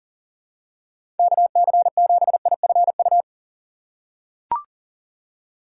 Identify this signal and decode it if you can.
Morse code, heard as a keyed tone, about 37 wpm, 700 Hz. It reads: KC7IFU
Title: USA Call Signs - 6 Characters - 2x Character Spacing - 40wpm